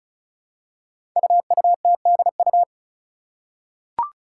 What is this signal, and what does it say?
Morse code, heard as a keyed tone, about 36 wpm, 700 Hz. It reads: UUTDU
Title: Code Groups - 5 Letters - 35wpm